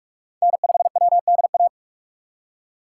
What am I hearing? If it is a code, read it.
Morse code, heard as a keyed tone, about 42 wpm, 700 Hz. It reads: N5WDA
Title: USA Call Signs - 4x Character Spacing - 45wpm